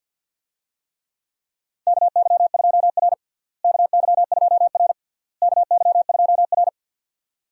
Morse code, heard as a keyed tone, about 52 wpm, 700 Hz. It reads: KY2R KY2R KY2R
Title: USA Call Signs - 4 Characters - Repeated 3x - 50wpm